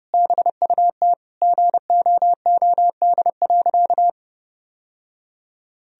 Morse code, heard as a keyed tone, about 30 wpm, 700 Hz. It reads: BUT GOOD.
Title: Sentences from Top 1000 Words - Review Part 2 - 30wpm